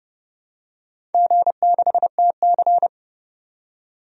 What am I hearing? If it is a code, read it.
Morse code, heard as a keyed tone, about 30 wpm, 700 Hz. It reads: G6TC
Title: DXCC-100 Call Signs - 4 Characters - 30wpm